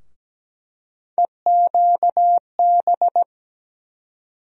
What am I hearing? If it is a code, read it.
Morse code, heard as a keyed tone, about 17 wpm, 700 Hz. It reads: EQB
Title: USA Call Sign Postfixes - 3 Letters 17wpm